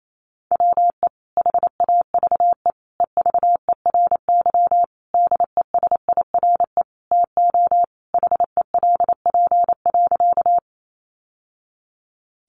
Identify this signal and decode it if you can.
Morse code, heard as a keyed tone, about 28 wpm, 700 Hz. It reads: WE HAVE EVERY DESIRE TO HELP.
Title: Sentences from Top 1100 Words 28wpm